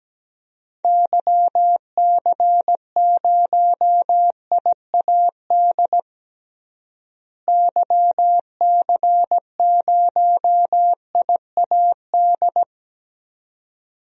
Morse code, heard as a keyed tone, about 17 wpm, 700 Hz. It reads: YC0IAD YC0IAD
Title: DXCC-100 Call Signs - 6 Characters - Repeated 2x - 17wpm